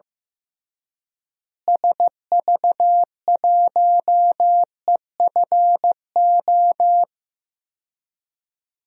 Morse code, heard as a keyed tone, about 15 wpm, 700 Hz. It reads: SV1EFO
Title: DXCC-100 Call Signs - 6 Characters - Repeated 3x - 15wpm